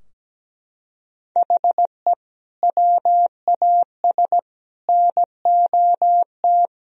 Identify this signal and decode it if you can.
Morse code, heard as a keyed tone, about 17 wpm, 700 Hz. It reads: HE WAS NOT
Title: Sentences from Top 1000 Words - Review Part 3 - 17wpm